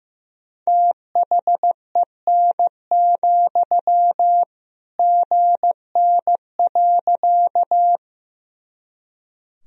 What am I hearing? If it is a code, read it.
Morse code, heard as a keyed tone, about 15 wpm, 700 Hz. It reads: THEN, GN.
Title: Sentences from Top 300 Words and Abbreviations 15wpm